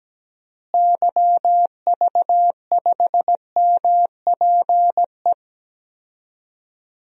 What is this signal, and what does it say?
Morse code, heard as a keyed tone, about 17 wpm, 700 Hz. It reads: YV5MPE
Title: DXCC-100 Call Signs - 6 Characters - Repeated 3x - 17wpm